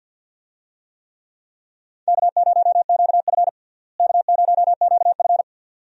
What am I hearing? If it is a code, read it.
Morse code, heard as a keyed tone, about 51 wpm, 700 Hz. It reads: K0QF K0QF
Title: USA Call Signs - 4 Characters - Repeated 2x - 50wpm